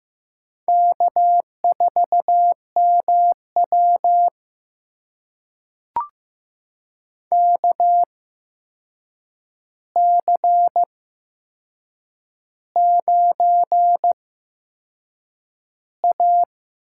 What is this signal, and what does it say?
Morse code, heard as a keyed tone, about 15 wpm, spaced as for 6 wpm, 700 Hz. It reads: K4MW KC9A
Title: USA Call Signs - 4 Characters - 8x Character Spacing - 15wpm